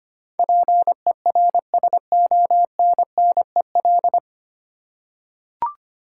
Morse code, heard as a keyed tone, about 25 wpm, 700 Hz. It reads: PERSONNEL
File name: 9 Letter Words - First 3 Letters Given - 25wpm